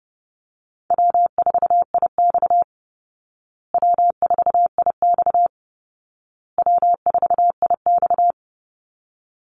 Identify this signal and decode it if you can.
Morse code, heard as a keyed tone, about 30 wpm, 700 Hz. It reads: W4IX W4IX W4IX
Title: Easy Call Signs Repeated 3x - 30wpm